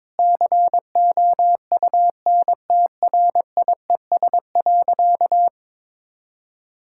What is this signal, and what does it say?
Morse code, heard as a keyed tone, about 22 wpm, 700 Hz. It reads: COUNTRIES.
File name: Difficult Sentences with Hints - 22wpm